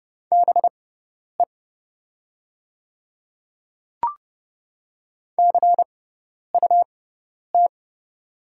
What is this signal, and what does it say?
Morse code, heard as a keyed tone, about 30 wpm, spaced as for 10 wpm, 700 Hz. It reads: BE CUT
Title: 6x Character Spacing - Top 500 Words - 30wpm